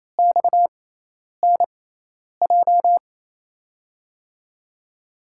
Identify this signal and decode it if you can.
Morse code, heard as a keyed tone, about 28 wpm, spaced as for 10 wpm, 700 Hz. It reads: XNJ
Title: USA Call Signs - 6x Character Spacing - 28wpm